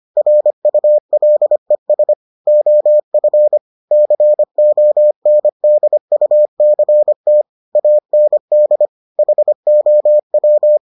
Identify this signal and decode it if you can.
Morse code, heard as a keyed tone, about 25 wpm, 600 Hz. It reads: RULES OF CONDUCT AND HOW